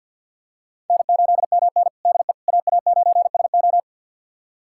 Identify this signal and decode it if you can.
Morse code, heard as a keyed tone, about 46 wpm, 700 Hz. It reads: N8MN DE AA9SO
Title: USA Callsign DE Callsign 50wpm